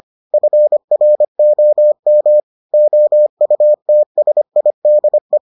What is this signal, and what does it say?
Morse code, heard as a keyed tone, about 25 wpm, 600 Hz. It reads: FROM OUTSIDE